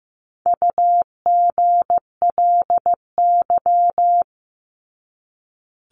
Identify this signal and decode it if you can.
Morse code, heard as a keyed tone, about 15 wpm, 700 Hz. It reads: UGLY
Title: Top 1751-2000 Words 15wpm